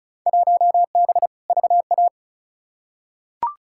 Morse code, heard as a keyed tone, about 36 wpm, 700 Hz. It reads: 1B VA
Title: Field Day Exchanges Encore - 35wpm